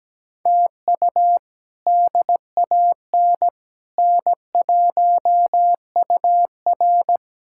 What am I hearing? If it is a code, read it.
Morse code, heard as a keyed tone, about 17 wpm, 700 Hz. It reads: TU DAN N1UR
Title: K1USN SST QSO Practice - 17wpm